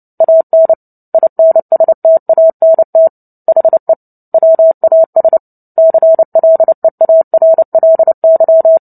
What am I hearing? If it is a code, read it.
Morse code, heard as a keyed tone, about 30 wpm, 650 Hz. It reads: AN INSTANT HE WAS CLEARLY
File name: Book Two / 03-Chapter 3 - 29 WPM